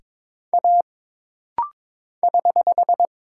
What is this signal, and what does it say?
Morse code, heard as a keyed tone, about 22 wpm, 700 Hz. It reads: A <HH>
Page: This rapid-fire timing is geared to push your ICR (Instant Character Recognition) skills.